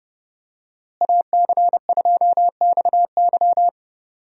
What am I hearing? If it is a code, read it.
Morse code, heard as a keyed tone, about 30 wpm, 700 Hz. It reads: AC2XY
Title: USA Call Signs - 5 Characters - Repeated 2x - 30wpm